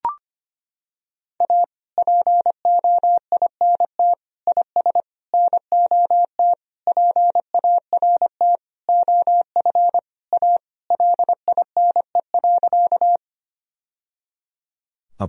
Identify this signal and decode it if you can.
Morse code, heard as a keyed tone, about 25 wpm, 700 Hz. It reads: A POINT IS NOT PART OF A LINE.
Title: Sentences from Top 1000 Words - Review Part 1 - 25wpm